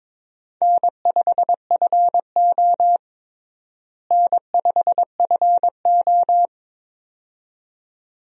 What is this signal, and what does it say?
Morse code, heard as a keyed tone, about 22 wpm, 700 Hz. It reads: N5FO N5FO
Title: USA Call Signs - 4 Characters - Repeated 2x - 22wpm